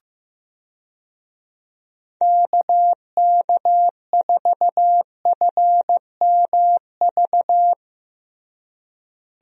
Morse code, heard as a keyed tone, about 15 wpm, 700 Hz. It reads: KK4FMV
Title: USA Call Signs - 6 Characters - Repeated 2x - 15wpm